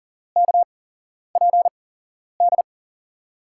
Morse code, heard as a keyed tone, about 40 wpm, spaced as for 11 wpm, 700 Hz. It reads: KPD
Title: USA Call Signs - 6 Characters - 8x Character Spacing - 40wpm